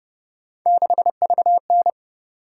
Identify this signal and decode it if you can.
Morse code, heard as a keyed tone, about 30 wpm, 700 Hz. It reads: 6VN